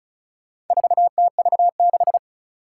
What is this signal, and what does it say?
Morse code, heard as a keyed tone, about 36 wpm, 700 Hz. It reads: 4TV6
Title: Four Challenging Letter-Number Combinations 35wpm